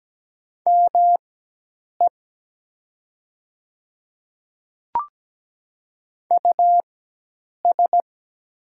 Morse code, heard as a keyed tone, about 17 wpm, spaced as for 8 wpm, 700 Hz. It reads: ME US
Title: 4x Character Spacing - Top 200 Words - 17wpm